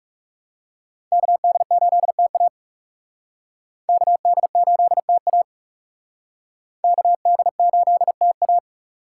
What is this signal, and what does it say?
Morse code, heard as a keyed tone, about 37 wpm, 700 Hz. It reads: KD8TA KD8TA KD8TA
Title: Speed Racing - USA Call Signs 30wpm